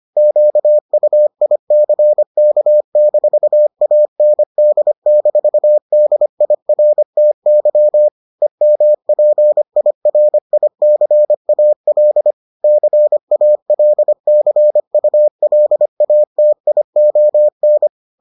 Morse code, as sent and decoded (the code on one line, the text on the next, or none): --.- ..- .. -.-. -.- -....- .- -. -.. -....- -.. .. .-. - -.-- / . -- .--. .. .-. .. -.-. .- .-.. / -.-. .- .-.. -.-. ..- .-.. .- - .. --- -.
QUICK-AND-DIRTY EMPIRICAL CALCULATION